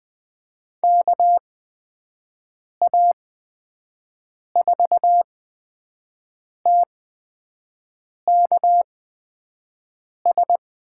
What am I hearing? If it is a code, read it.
Morse code, heard as a keyed tone, about 20 wpm, spaced as for 5 wpm, 700 Hz. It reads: KA4TKS
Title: USA Call Signs - 8x Character Spacing - 20wpm